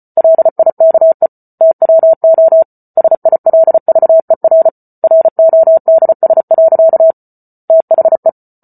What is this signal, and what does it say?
Morse code, heard as a keyed tone, about 34 wpm, 650 Hz. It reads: LIKE TWO SILVER RODS. THE